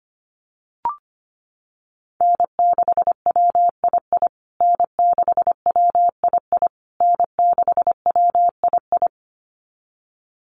Morse code, heard as a keyed tone, about 25 wpm, 700 Hz. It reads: N6WII N6WII N6WII
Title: USA Call Signs Repeated 3x 25wpm